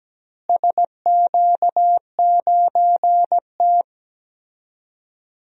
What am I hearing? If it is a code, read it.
Morse code, heard as a keyed tone, about 17 wpm, 700 Hz. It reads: SQ9T
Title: DXCC-100 Call Signs - 4 Characters - Repeated 3x - 17wpm